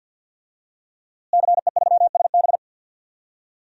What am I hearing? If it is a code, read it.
Morse code, heard as a keyed tone, about 46 wpm, 700 Hz. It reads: CE3SB
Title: DXCC-100 Call Signs - 5 Characters - Repeated 2x - 50wpm